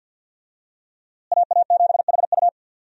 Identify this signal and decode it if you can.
Morse code, heard as a keyed tone, about 52 wpm, 700 Hz. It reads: AA7HU
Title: USA Call Signs - 5 Characters - Repeated 2x - 50wpm